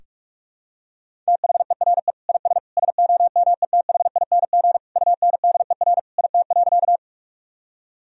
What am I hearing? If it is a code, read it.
Morse code, heard as a keyed tone, about 43 wpm, 700 Hz. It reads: THERE IS SOMETHING UNDER IT.
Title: Sentences from Top 1000 Words - Review Part 1 - 45wpm